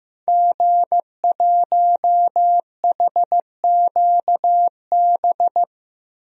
Morse code, heard as a keyed tone, about 15 wpm, 700 Hz. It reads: G1HQB
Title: DXCC-100 Call Signs - 5 Characters - Repeated 3x - 15wpm